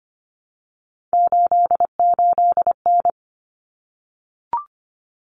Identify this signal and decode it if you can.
Morse code, heard as a keyed tone, about 25 wpm, 700 Hz. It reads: 88N
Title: Common Cut Numbers - 3-Digit 25wpm